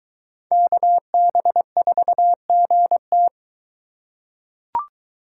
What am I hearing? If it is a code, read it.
Morse code, heard as a keyed tone, about 22 wpm, 700 Hz. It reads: KB4GT